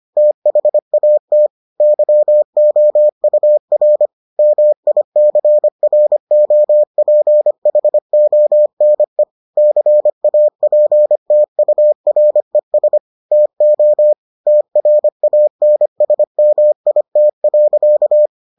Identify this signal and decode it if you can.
Morse code, heard as a keyed tone, about 25 wpm, 600 Hz. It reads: THAT YOUR MICROPHONE CAPTURES TO TRANSMIT.